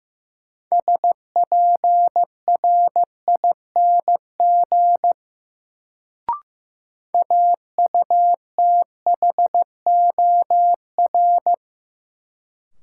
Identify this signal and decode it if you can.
Morse code, heard as a keyed tone, about 16 wpm, 700 Hz. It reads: SPRING AUTHOR
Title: Sentences from Top 1200 Words - New Word Drill - 15wpm